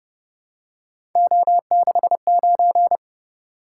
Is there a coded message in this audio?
Yes: Morse code from 1.2 to 3.0 s, heard as a keyed tone, about 29 wpm, 700 Hz. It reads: O69